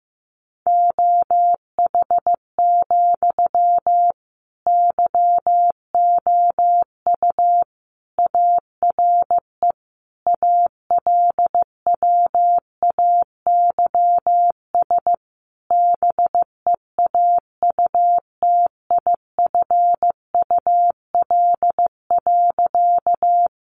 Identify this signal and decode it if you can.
Morse code, heard as a keyed tone, about 15 wpm, 700 Hz. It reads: OH, YOU ARE ALWAYS BEAUTIFUL.
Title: Sentences from Top 900 Words 15wpm